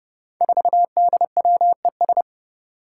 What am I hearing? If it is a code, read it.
Morse code, heard as a keyed tone, about 30 wpm, 700 Hz. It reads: 4DWES
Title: Five Challenging Letter-Number Combinations 30wpm